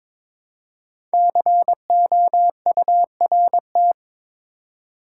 Morse code, heard as a keyed tone, about 22 wpm, 700 Hz. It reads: COURT
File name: 5 Letter Words - First 3 Letters Given - 22wpm